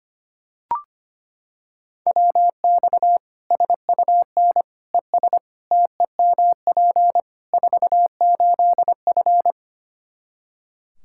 Morse code, heard as a keyed tone, about 25 wpm, 700 Hz. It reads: WX SUN ES TEMP 48F
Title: K1USN SST QSO Practice 25wpm